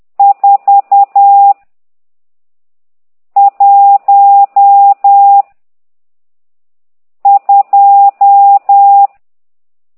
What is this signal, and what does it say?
Morse code, heard as a keyed tone, about 10 wpm, 800 Hz. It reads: 4 1 2
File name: ct2idw / exerc12